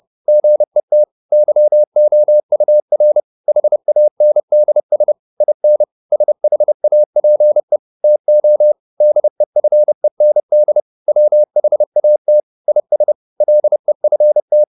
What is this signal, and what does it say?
Morse code, heard as a keyed tone, about 30 wpm, 600 Hz. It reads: GET YOUR HANDS IN SHAPE TO DEFEND WHAT IS LEFT